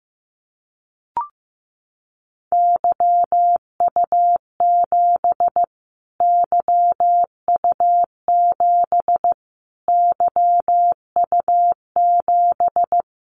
Morse code, heard as a keyed tone, about 15 wpm, 700 Hz. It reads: YU7 YU7 YU7
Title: Callsign Prefixes for Top 25 DXCC - Repeated 3x - 15wpm